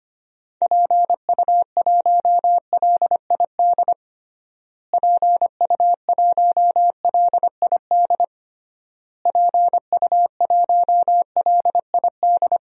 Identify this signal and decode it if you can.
Morse code, heard as a keyed tone, about 25 wpm, 690 Hz. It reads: PU1LID PU1LID PU1LID
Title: DXCC-100 Call Signs - 6 Characters - Repeated 3x - 25wpm